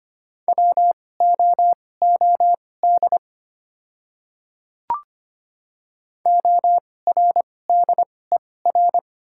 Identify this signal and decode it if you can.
Morse code, heard as a keyed tone, about 25 wpm, spaced as for 18 wpm, 690 Hz. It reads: WOOD ORDER
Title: 2x Character Spacing - Top 500 Words - 25wpm